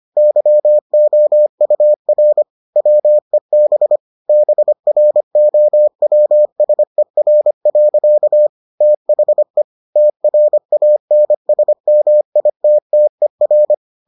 Morse code, heard as a keyed tone, about 25 wpm, 600 Hz. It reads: YOUR WEB BROWSER. THE TRANSMITTER